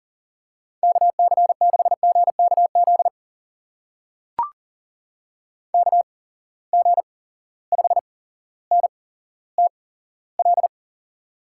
Morse code, heard as a keyed tone, about 40 wpm, spaced as for 17 wpm, 700 Hz. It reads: KC6GKZ KG5NTL
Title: USA Call Signs - 6 Characters - 8x Character Spacing - 40wpm